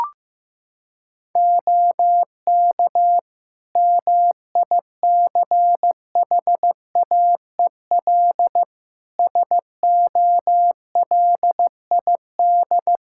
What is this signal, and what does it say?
Morse code, heard as a keyed tone, about 15 wpm, 700 Hz. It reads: OK MICHAEL SOLID
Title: Practice QSO - Exchange 3 - More Info - 15wpm